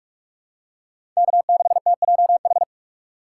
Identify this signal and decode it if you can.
Morse code, heard as a keyed tone, about 47 wpm, 690 Hz. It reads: K6TJH
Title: USA Call Signs - 5 Characters - 2x Character Spacing - 45wpm